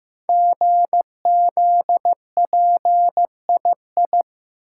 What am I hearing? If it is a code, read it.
Morse code, heard as a keyed tone, about 15 wpm, 700 Hz. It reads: GZPII
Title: Code Groups - 5 Letter:Numbers - 15wpm